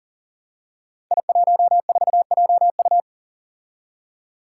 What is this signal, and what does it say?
Morse code, heard as a keyed tone, about 38 wpm, 700 Hz. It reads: I14JU
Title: Five Challenging Letter-Number Combinations 40wpm